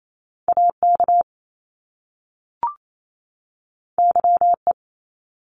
Morse code, heard as a keyed tone, about 28 wpm, 700 Hz. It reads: AK YE
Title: Top Two Letter Combinations in Words 28wpm